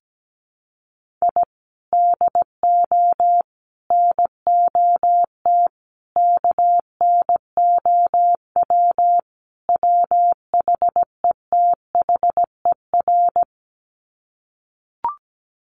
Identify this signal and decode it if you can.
Morse code, heard as a keyed tone, about 17 wpm, 700 Hz. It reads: I DO NOT KNOW WHETHER
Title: Sets of 5 Words 17wpm